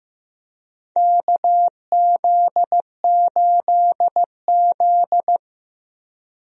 Morse code, heard as a keyed tone, about 15 wpm, 700 Hz. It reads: KZ8Z
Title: USA Call Signs - 4 Characters - 15wpm